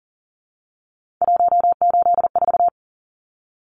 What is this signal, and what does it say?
Morse code, heard as a keyed tone, about 40 wpm, 700 Hz. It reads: 184
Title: Numbers - 3-Digit 40wpm